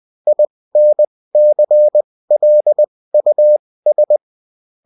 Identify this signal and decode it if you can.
Morse code, heard as a keyed tone, about 20 wpm, spaced as for 16 wpm, 600 Hz. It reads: INCLUS